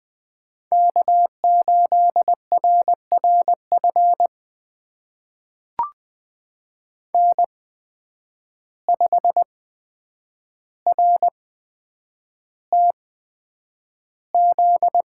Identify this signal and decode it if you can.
Morse code, heard as a keyed tone, about 20 wpm, spaced as for 9 wpm, 700 Hz. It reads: K8RRF N5RTZ
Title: USA Call Signs - 5 Characters - 8x Character Spacing - 20wpm